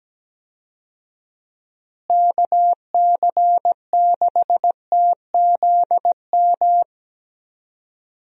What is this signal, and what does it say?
Morse code, heard as a keyed tone, about 17 wpm, 700 Hz. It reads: KC6TZM
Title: USA Call Signs - 6 Characters - Repeated 2x - 17wpm